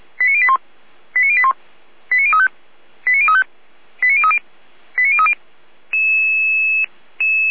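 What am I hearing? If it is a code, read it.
Five-tone selective call (ZVEI), heard as quick runs of five tones, tones of about 70 ms. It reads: 89081 89081 89035 89035 89030 89030
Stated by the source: ZVEI five-tone selective call received on SDR.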